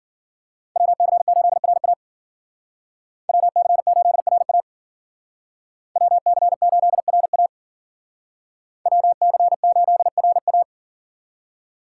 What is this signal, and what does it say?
Morse code, heard as a keyed tone, about 52 wpm, 700 Hz. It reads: WC8RA WC8RA WC8RA WC8RA